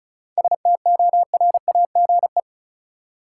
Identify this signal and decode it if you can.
Morse code, heard as a keyed tone, about 36 wpm, 700 Hz. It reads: STORAGE
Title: Seven Letter Words - 35wpm